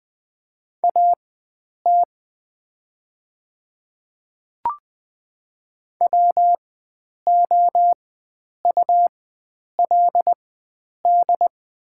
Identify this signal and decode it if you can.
Morse code, heard as a keyed tone, about 20 wpm, spaced as for 9 wpm, 700 Hz. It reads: AT WOULD
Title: 4x Character Spacing - Top 100 Words - 20wpm